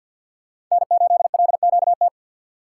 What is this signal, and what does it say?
Morse code, heard as a keyed tone, about 53 wpm, 700 Hz. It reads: N8LQT
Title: USA Call Signs - 5 Characters - 3x Character Spacing - 50wpm